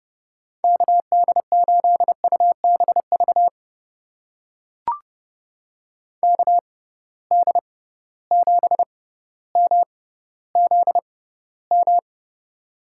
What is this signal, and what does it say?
Morse code, heard as a keyed tone, about 30 wpm, spaced as for 15 wpm, 700 Hz. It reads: KD8UBV KD7MZM
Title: USA Call Signs - 6 Characters - 6x Character Spacing - 30wpm